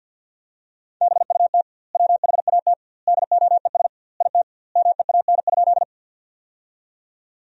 Morse code, heard as a keyed tone, about 50 wpm, 700 Hz. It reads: BUT WHAT DOES IT MEAN?
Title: 2x Word-Spacing - Sentences from Top 200 Words - 50wpm